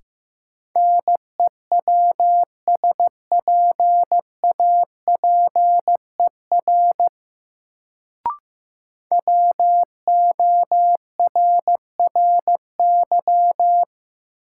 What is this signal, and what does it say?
Morse code, heard as a keyed tone, about 15 wpm, 700 Hz. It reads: NEWSPAPER WORRY
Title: Sentences from Top 1100 Words - New Word Drill Alt - 15wpm